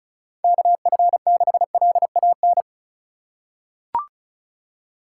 Morse code, heard as a keyed tone, about 36 wpm, 700 Hz. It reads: KF6LAN
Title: USA Call Signs - 6 Characters - 3x Character Spacing - 35wpm